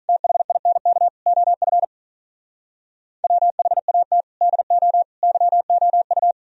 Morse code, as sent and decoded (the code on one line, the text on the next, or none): - .... .. -. -.- / --- ..-. / .-- .... .- - / -.. --- / -.-- --- ..-
THINK OF WHAT DO YOU